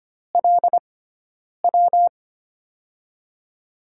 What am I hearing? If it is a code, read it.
Morse code, heard as a keyed tone, about 25 wpm, spaced as for 9 wpm, 700 Hz. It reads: LW
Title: USA Call Signs - 6x Character Spacing - 25wpm